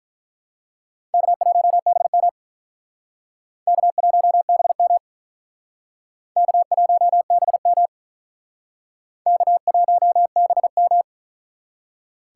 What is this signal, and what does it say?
Morse code, heard as a keyed tone, about 42 wpm, 700 Hz. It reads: K1BM K1BM K1BM K1BM
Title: Speed Racing - USA 1x2 Call Signs 35wpm